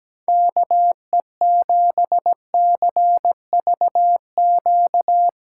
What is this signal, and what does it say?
Morse code, heard as a keyed tone, about 18 wpm, 700 Hz. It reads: KE7CVQ
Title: Speed Racing - USA 2x3 Call Signs 17wpm